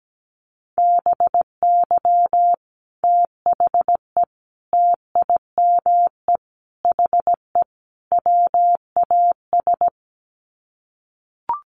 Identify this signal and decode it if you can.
Morse code, heard as a keyed tone, about 17 wpm, 690 Hz. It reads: BY THE TIME HE WAS
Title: Sets of 5 Words Encore 17wpm